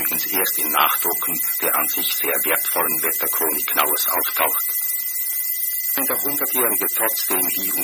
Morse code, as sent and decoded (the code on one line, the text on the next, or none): none